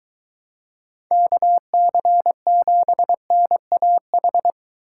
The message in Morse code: KC7NAH

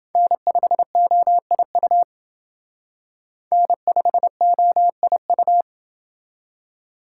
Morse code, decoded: N5OIU N5OIU

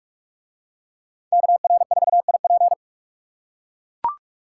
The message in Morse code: KR4IP

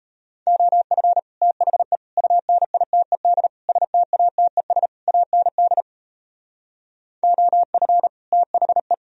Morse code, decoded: OF THE UNITED STATES AND OF THE

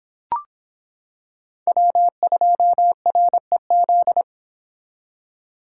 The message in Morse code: W2REZ

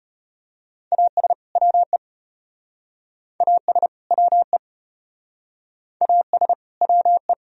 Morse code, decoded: AS WE AS WE AS WE

